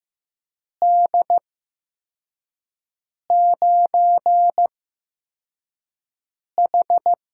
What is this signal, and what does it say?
Morse code, heard as a keyed tone, about 15 wpm, spaced as for 4 wpm, 700 Hz. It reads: D9H
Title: USA Call Signs - 4 Characters - 8x Character Spacing - 15wpm